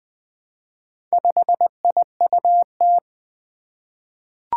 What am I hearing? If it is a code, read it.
Morse code, heard as a keyed tone, about 20 wpm, 700 Hz. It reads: 5IUT
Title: Four Challenging Letter-Number Combinations 20wpm